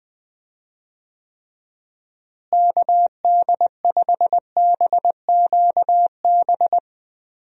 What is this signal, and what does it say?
Morse code, heard as a keyed tone, about 20 wpm, 700 Hz. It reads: KD5BQB
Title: USA Call Signs - 6 Characters - Repeated 2x - 20wpm